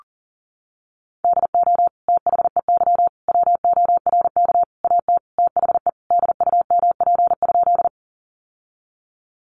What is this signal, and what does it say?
Morse code, heard as a keyed tone, about 40 wpm, 700 Hz. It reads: DO THEY WORK AT THE DUMP?
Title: Sentences from Top 1300 Words 40wpm